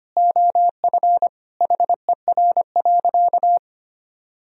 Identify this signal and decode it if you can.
Morse code, heard as a keyed tone, about 25 wpm, 700 Hz. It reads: OF HER.
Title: Sentences from Top 1000 Words - Review Part 4 - 25wpm